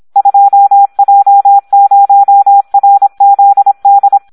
Morse code, heard as a keyed tone, about 26 wpm, 800 Hz, with radio noise. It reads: 2J0RZD
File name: Audio practica CW / SD 1 26